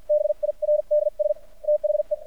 Morse code, heard as a keyed tone, about 53 wpm, spaced as for 43 wpm, 600 Hz. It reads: BIRDS THE
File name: archived / CRSnet